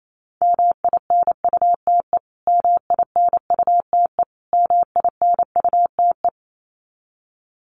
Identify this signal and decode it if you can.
Morse code, heard as a keyed tone, about 28 wpm, 700 Hz. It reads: MINUTE MINUTE MINUTE
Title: Top 500 Words Repeated 28wpm